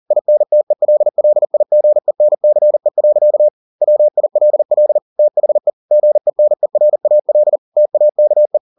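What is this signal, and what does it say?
Morse code, heard as a keyed tone, about 40 wpm, 600 Hz. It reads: INTELLIGENCE. WILL THE GENERAL TAKE
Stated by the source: Morse code at 40 wpm, practice text, CW med Gnister.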